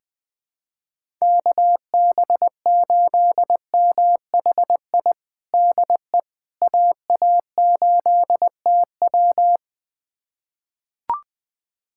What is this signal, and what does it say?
Morse code, heard as a keyed tone, about 21 wpm, 700 Hz. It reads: KB8MHI DE AA8TW
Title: USA Callsign DE Callsign 20wpm